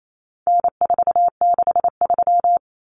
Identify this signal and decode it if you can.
Morse code, heard as a keyed tone, about 28 wpm, 700 Hz. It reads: N463
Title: Common Cut Numbers - 4-Digit 28wpm